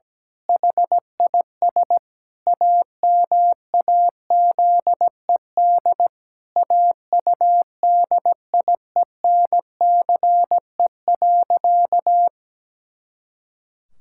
Morse code, heard as a keyed tone, about 17 wpm, 700 Hz. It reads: HIS AMAZED AUDIENCE.
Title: Pangram Challenge v2 - 17wpm